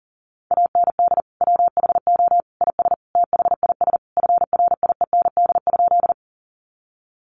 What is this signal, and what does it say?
Morse code, heard as a keyed tone, about 37 wpm, 700 Hz. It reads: AND WHO IS THIS FRIEND?
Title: Sentences from Top 300 Words 40wpm